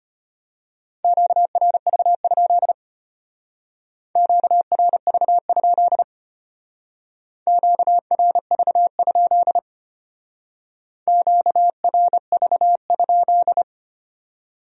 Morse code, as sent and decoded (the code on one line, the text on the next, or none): --.- .-. ...- ..--.. / --.- .-. ...- ..--.. / --.- .-. ...- ..--.. / --.- .-. ...- ..--..
QRV? QRV? QRV? QRV?